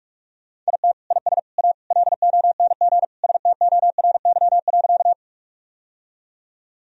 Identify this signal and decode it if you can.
Morse code, heard as a keyed tone, about 44 wpm, 700 Hz. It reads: IT IS A LONG STORY.
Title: Sentences from Top 1000 Words - Review Part 1 - 45wpm